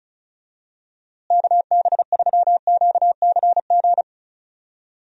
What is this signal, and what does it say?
Morse code, heard as a keyed tone, about 36 wpm, 700 Hz. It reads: KB3QCG
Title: USA Call Signs - 6 Characters - Repeated 3x - 35wpm